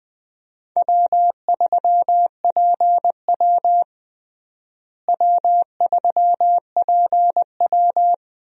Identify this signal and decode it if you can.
Morse code, heard as a keyed tone, about 21 wpm, 700 Hz. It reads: W3PW W3PW
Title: USA Call Signs - 4 Characters - Repeated 3x - 20wpm